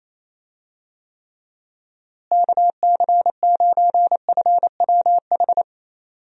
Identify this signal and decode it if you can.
Morse code, heard as a keyed tone, about 28 wpm, 700 Hz. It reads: KC9FWH
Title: USA Call Signs - 6 Characters - Repeated 2x - 28wpm